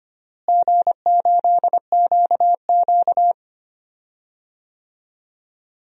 Morse code, heard as a keyed tone, about 25 wpm, 700 Hz. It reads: G8QQ
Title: DXCC-100 Call Signs - 4 Characters - Repeated 2x - 25wpm